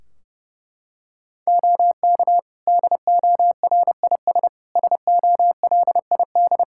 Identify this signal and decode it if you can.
Morse code, heard as a keyed tone, about 32 wpm, 700 Hz. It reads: OK DORIS SOLID
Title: Practice QSO - Exchange 3 - More Info - 30wpm